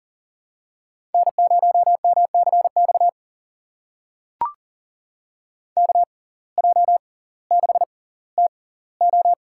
Morse code, heard as a keyed tone, about 40 wpm, spaced as for 20 wpm, 700 Hz. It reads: N0MCX KJ6TO
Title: USA Call Signs - 5 Characters - 6x Character Spacing - 40wpm